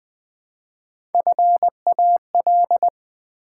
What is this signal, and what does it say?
Morse code, heard as a keyed tone, about 20 wpm, 700 Hz. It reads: FAL
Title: Build A Word - 1 Letter At A Time - 3 to 5-Letter Words - 20wpm